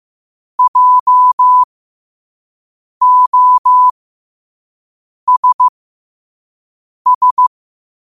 Morse code, heard as a keyed tone, about 15 wpm, spaced as for 5 wpm, 1000 Hz. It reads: JOSS
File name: qotdfiles / 5wpm